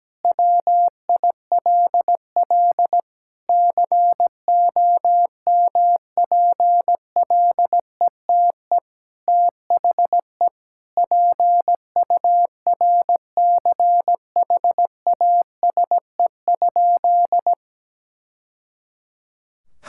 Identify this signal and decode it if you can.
Morse code, heard as a keyed tone, about 17 wpm, 700 Hz. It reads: WILL COMPLETE THE PURCHASE?
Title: Sentences from Top 1000 Words - Review Part 5 - 17wpm